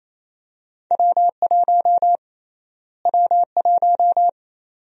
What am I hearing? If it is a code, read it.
Morse code, heard as a keyed tone, about 28 wpm, 700 Hz. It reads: W1 W1
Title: USA Call Sign Prefixes - Repeated 2x - 28wpm